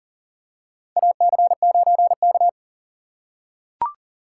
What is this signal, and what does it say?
Morse code, heard as a keyed tone, about 40 wpm, 700 Hz. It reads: AC9K
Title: USA Call Signs - 4x Character Spacing - 40wpm